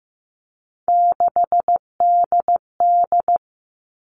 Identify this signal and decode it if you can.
Morse code, heard as a keyed tone, about 15 wpm, 700 Hz. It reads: 6DD
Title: All Cut Numbers - 3-Digit 15wpm